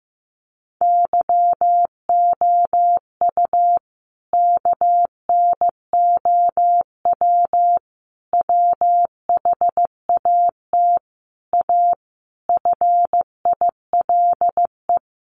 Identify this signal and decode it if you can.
Morse code, heard as a keyed tone, about 15 wpm, 700 Hz. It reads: YOU KNOW WHAT A FILE
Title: Sentences from Top 1200 Words 15wpm